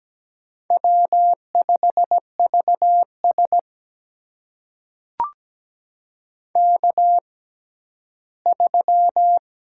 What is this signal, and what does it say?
Morse code, heard as a keyed tone, about 17 wpm, spaced as for 11 wpm, 700 Hz. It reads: W5VS K3
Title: USA Call Signs - 4 Characters - 6x Character Spacing - 17wpm